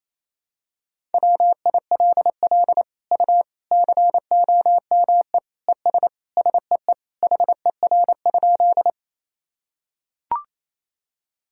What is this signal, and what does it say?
Morse code, heard as a keyed tone, about 28 wpm, 700 Hz. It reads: WILL U COME ES SEE HER?
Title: Sentences from Top 100 Words and Abbreviations 28wpm